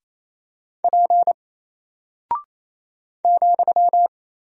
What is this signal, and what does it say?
Morse code, heard as a keyed tone, about 27 wpm, 700 Hz. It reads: P ,
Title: Single Character (Letter:Number:Punctuation:Prosign) Alt 28wpm